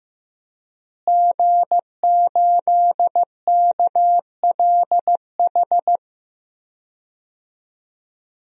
Morse code, heard as a keyed tone, about 15 wpm, 700 Hz. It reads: G8KLH
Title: DXCC-100 Call Signs - 5 Characters - Repeated 3x - 15wpm